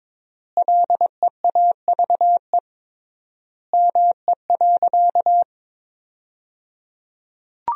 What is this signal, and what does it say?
Morse code, heard as a keyed tone, about 22 wpm, 700 Hz. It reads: LEAVE ME.
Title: 3x Word-Spacing - Sentences from Top 400 Words - 22wpm